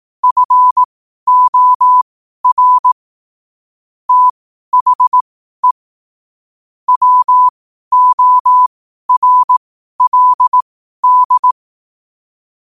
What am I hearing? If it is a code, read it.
Morse code, heard as a keyed tone, about 18 wpm, spaced as for 12 wpm, 1000 Hz. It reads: FOR THE WORLD
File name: qotdfiles / 12wpm